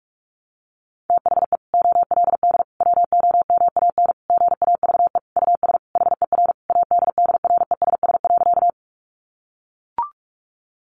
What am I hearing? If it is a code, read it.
Morse code, heard as a keyed tone, about 44 wpm, 700 Hz. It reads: THE OLD WOMAN GAVE US HER ADDRESS.
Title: Sentences from Top 1100 Words 45wpm